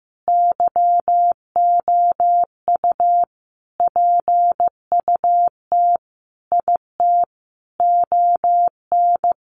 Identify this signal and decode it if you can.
Morse code, heard as a keyed tone, about 15 wpm, 700 Hz. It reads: YOU PUT IT ON
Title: Sentences from Top 1400 Words 15wpm